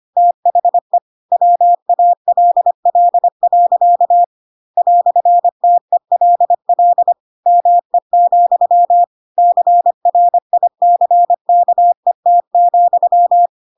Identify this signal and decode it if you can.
Morse code, heard as a keyed tone, about 25 wpm, 700 Hz. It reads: THE WALL. "TELL ME, CRICKET,
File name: Pinocchio / Pinocchio ch4 25 25 ik1hge